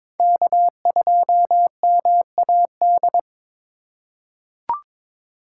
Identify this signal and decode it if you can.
Morse code, heard as a keyed tone, about 22 wpm, 700 Hz. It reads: K2MAD
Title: USA Call Signs - 5 Characters - 2x Character Spacing - 22wpm